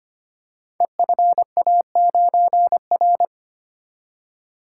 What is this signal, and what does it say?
Morse code, heard as a keyed tone, about 25 wpm, 700 Hz. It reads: EFA9R
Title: Code Groups - 5 Letter:Numbers - 25wpm